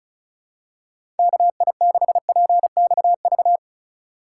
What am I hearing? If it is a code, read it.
Morse code, heard as a keyed tone, about 36 wpm, 700 Hz. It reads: KI6PXV
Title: USA Call Signs - 6 Characters - 2x Character Spacing - 35wpm